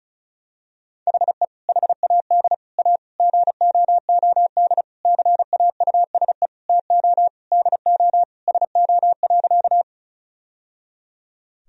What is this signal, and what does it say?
Morse code, heard as a keyed tone, about 36 wpm, 700 Hz. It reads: HE HAD A GOOD CAUSE TO DO SO.
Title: Sentences from Top 1000 Words - Review Part 1 - 35wpm